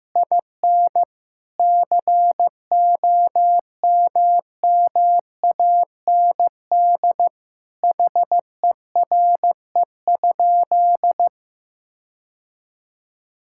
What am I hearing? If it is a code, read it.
Morse code, heard as a keyed tone, about 16 wpm, 700 Hz. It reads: IN COMMAND HERE?
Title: Sentences from Top 1600 Words 15wpm